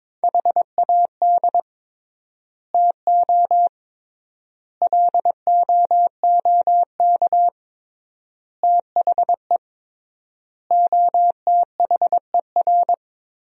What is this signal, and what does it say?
Morse code, heard as a keyed tone, about 21 wpm, 700 Hz. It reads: HAD TO LOOK THE OTHER
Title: 3x Word-Spacing - Sentences from Top 100 Words - 22wpm